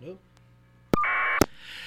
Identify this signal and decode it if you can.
Caller ID data burst, with speech.